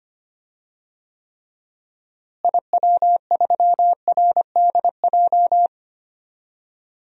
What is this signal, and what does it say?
Morse code, heard as a keyed tone, about 25 wpm, 700 Hz. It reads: IW3RDJ
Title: DXCC-100 Call Signs - 6 Characters - Repeated 2x - 25wpm